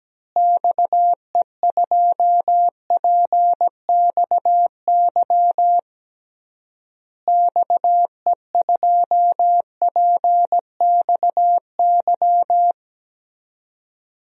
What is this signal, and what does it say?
Morse code, heard as a keyed tone, about 17 wpm, 700 Hz. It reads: XE2PXY XE2PXY
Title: DXCC-100 Call Signs - 6 Characters - Repeated 3x - 17wpm